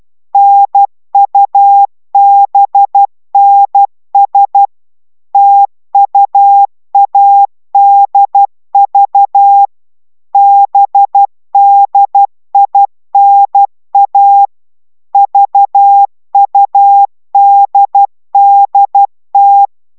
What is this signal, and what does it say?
Morse code, heard as a keyed tone, about 12 wpm, 800 Hz. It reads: NUBNS TUADV BDINA VUDDT